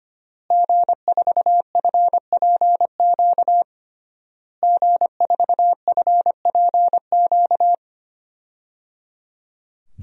Morse code, heard as a keyed tone, about 25 wpm, 700 Hz. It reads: G4FPQ G4FPQ
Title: DXCC-100 Call Signs - 5 Characters - Repeated 3x - 25wpm